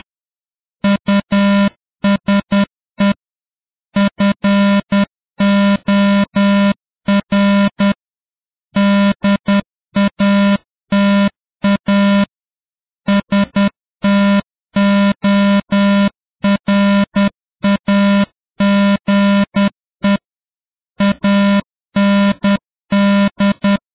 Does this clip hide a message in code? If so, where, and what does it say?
Yes: Morse code from 0.8 to 23.8 s, heard as a keyed tone, about 10 wpm, 200 Hz. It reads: USE FOR DATA STORAGE AND